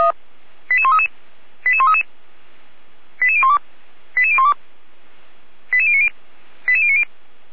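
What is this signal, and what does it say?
Five-tone selective call (ZVEI), heard as quick runs of five tones, tones of about 80 ms. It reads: 88133 88133 88012 88012 88090 88090
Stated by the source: ZVEI five-tone selective call 88501 received on an SDR by Feuerwehr Orlamünde.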